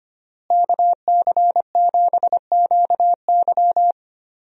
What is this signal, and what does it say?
Morse code, heard as a keyed tone, about 25 wpm, 700 Hz. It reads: KC7QY
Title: USA Call Signs - 5 Characters - Repeated 2x - 25wpm